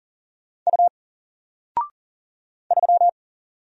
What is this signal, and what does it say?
Morse code, heard as a keyed tone, about 40 wpm, 700 Hz. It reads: U 3
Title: Single Letter-Number Alt 40wpm